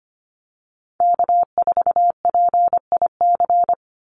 Morse code, heard as a keyed tone, about 25 wpm, 700 Hz. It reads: K4PIC
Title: NA SOTA Chaser Callsigns 25wpm